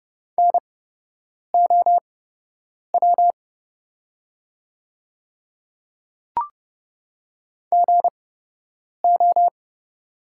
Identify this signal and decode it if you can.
Morse code, heard as a keyed tone, about 30 wpm, spaced as for 8 wpm, 700 Hz. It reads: NOW GO
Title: 8x Character Spacing - Top 100 Words - 30wpm